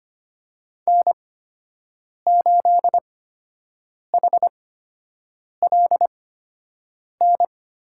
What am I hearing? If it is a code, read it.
Morse code, heard as a keyed tone, about 25 wpm, spaced as for 7 wpm, 700 Hz. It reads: N8HLN
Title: USA Call Signs - 8x Character Spacing - 25wpm